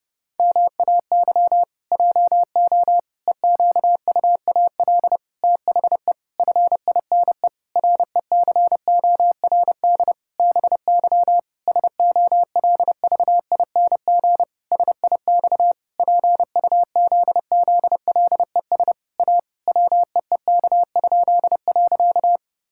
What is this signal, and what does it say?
Morse code, heard as a keyed tone, about 32 wpm, 700 Hz. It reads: MAY JO EQUAL THE FINE RECORD BY SOLVING SIX PUZZLES A WEEK?.
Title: Pangram Challenge v2 - 30wpm